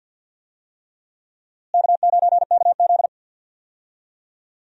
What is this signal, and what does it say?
Morse code, heard as a keyed tone, about 47 wpm, 700 Hz. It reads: K9KZ
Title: USA Call Signs - 4 Characters - Repeated 3x - 50wpm